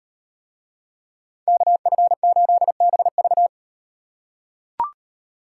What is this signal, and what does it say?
Morse code, heard as a keyed tone, about 38 wpm, 700 Hz. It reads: KF8BV